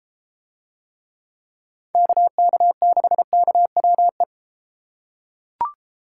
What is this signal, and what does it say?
Morse code, heard as a keyed tone, about 33 wpm, 700 Hz. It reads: KK6KWE